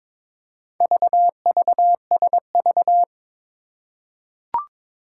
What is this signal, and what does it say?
Morse code, heard as a keyed tone, about 23 wpm, 700 Hz. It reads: VVSV
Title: Four Challenging Letter-Number Combinations 22wpm